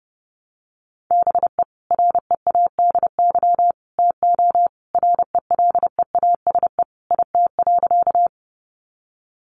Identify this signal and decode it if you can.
Morse code, heard as a keyed tone, about 30 wpm, 700 Hz. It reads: BE READY TO RELEASE IT.
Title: Sentences from Top 1100 Words 30wpm